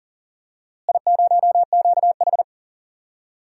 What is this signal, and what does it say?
Morse code, heard as a keyed tone, about 40 wpm, 700 Hz. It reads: I0QH
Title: DXCC-100 Call Signs - 4 Characters - Repeated 3x - 40wpm